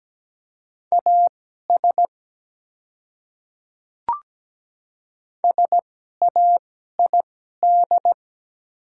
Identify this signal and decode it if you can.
Morse code, heard as a keyed tone, about 17 wpm, spaced as for 12 wpm, 700 Hz. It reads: AS SAID
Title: 2x Character Spacing - Top 100 Words - 17wpm